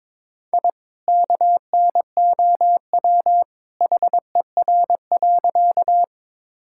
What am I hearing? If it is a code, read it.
Morse code, heard as a keyed tone, about 21 wpm, 700 Hz. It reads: I KNOW HER.
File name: practice / 22wpm / Sentences with 2-Word Phrases - 22wpm